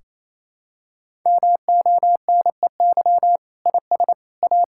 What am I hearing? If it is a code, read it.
Morse code, heard as a keyed tone, about 28 wpm, 700 Hz. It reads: MONEY IS A
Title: Sentences from Top 1600 Words 28wpm